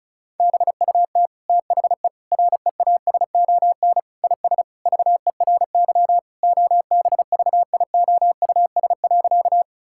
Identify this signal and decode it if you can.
Morse code, heard as a keyed tone, about 36 wpm, 700 Hz. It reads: BUT THE REASON IS VERY OBVIOUS.
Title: Sentences from Top 1600 Words 35wpm